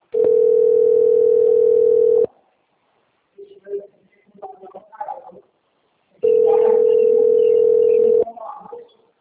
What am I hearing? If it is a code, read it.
Ringback tone (North America), 440 + 480 Hz.